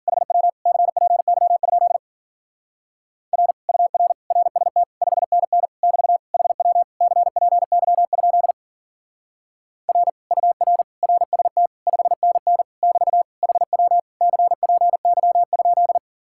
Morse code, heard as a keyed tone, about 51 wpm, 700 Hz. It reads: HW CPY? R UR RST 5NN <BT> HW CPY? R UR RST 5NN <BT> HW CPY?